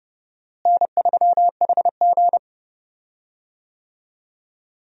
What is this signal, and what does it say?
Morse code, heard as a keyed tone, about 30 wpm, 700 Hz. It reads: N3HG
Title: USA Call Signs - 4 Characters - Repeated 3x - 30wpm